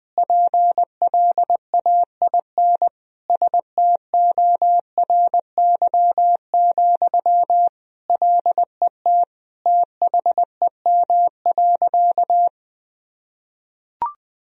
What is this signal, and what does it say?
Morse code, heard as a keyed tone, about 21 wpm, 700 Hz. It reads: PLAIN STORY, LET THEM.
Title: Sentences from Top 1000 Words - Review Part 2 - 20wpm